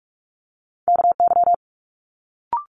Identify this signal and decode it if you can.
Morse code, heard as a keyed tone, about 46 wpm, 700 Hz. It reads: KY
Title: Multi-letter Phonemes Quick Fire 45wpm